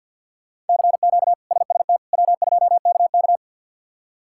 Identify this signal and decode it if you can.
Morse code, heard as a keyed tone, about 51 wpm, 690 Hz. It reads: CQ SST W2KK
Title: K1USN SST QSO Practice - 50wpm